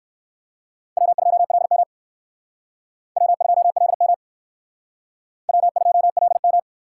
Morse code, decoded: W2LM W2LM W2LM

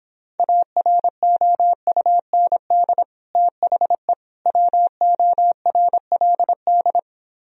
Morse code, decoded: AROUND THE WORLD

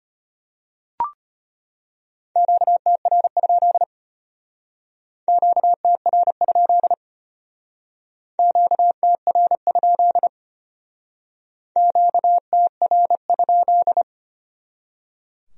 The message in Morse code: QTR? QTR? QTR? QTR?

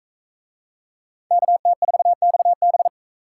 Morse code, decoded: KT4XB

